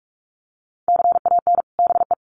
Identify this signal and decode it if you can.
Morse code, heard as a keyed tone, about 44 wpm, 700 Hz. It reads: CAN BE
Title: Sets of 2 Words 45wpm